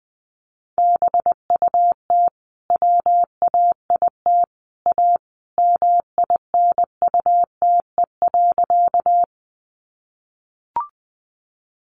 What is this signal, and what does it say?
Morse code, heard as a keyed tone, about 20 wpm, 700 Hz. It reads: BUT WAIT A MINUTE.
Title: Sentences from Top 500 Words 20wpm